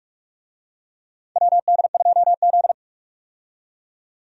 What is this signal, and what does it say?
Morse code, heard as a keyed tone, about 43 wpm, 700 Hz. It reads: WD2Z
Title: USA Call Signs - 4 Characters - 45wpm